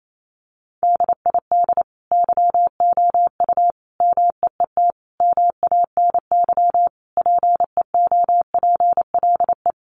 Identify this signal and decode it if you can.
Morse code, heard as a keyed tone, about 28 wpm, 700 Hz. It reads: DID YOU MEET MANY PEOPLE
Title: Sentences from Top 1100 Words 28wpm